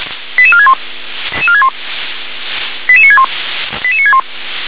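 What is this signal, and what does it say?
Five-tone selective call (ZVEI), heard as quick runs of five tones, tones of about 70 ms. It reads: 88461 88961 88961
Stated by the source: ZVEI five-tone selective call received on an SDR.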